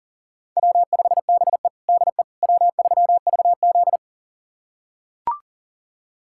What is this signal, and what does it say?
Morse code, heard as a keyed tone, about 40 wpm, 700 Hz. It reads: W5BE DE W3VZ
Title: USA Callsign DE Callsign - 4 Characters - 40wpm